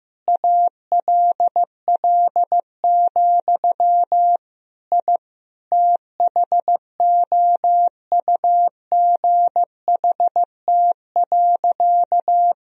Morse code, heard as a keyed tone, about 15 wpm, 700 Hz. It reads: ALL, I THOUGHT.
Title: Sentences from Top 1000 Words - Review Part 1 - 15wpm